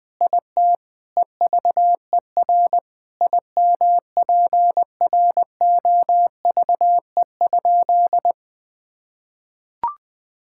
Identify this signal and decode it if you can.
Morse code, heard as a keyed tone, about 20 wpm, 700 Hz. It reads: IT EVER IMPROVE?
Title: Sentences from Top 1000 Words - Review Part 5 - 20wpm